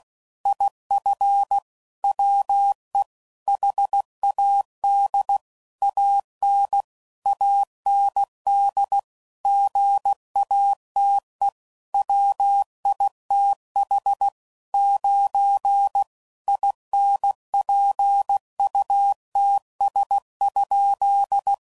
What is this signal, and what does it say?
Morse code, heard as a keyed tone, about 16 wpm, 800 Hz. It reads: IF WE HAD AN AND GATE WITH 9 INPUTS?
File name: CW Code Groups @15WPM / Technical Subjects / AND Gate